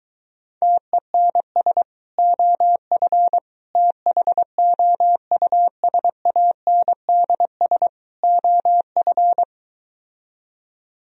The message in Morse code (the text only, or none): TENS OF THOUSANDS OF